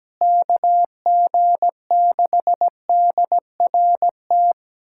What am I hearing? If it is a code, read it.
Morse code, heard as a keyed tone, about 17 wpm, 700 Hz. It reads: KG6DRT
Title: USA Callsign DE Callsign - 6 Characters - 17wpm